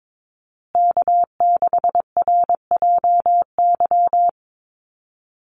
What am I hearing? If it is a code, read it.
Morse code, heard as a keyed tone, about 22 wpm, 700 Hz. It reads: K6RJY
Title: USA Call Signs Repeated 2x 22wpm